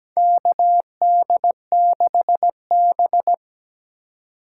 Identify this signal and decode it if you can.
Morse code, heard as a keyed tone, about 17 wpm, 700 Hz. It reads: KD6B
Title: USA Call Signs - 3x Character Spacing - 17wpm